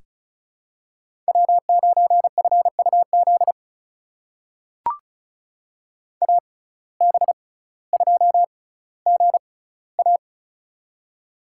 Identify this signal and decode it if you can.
Morse code, heard as a keyed tone, about 36 wpm, spaced as for 18 wpm, 700 Hz. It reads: W9FUZ AB2GA
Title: USA Call Signs - 5 Characters - 6x Character Spacing - 35wpm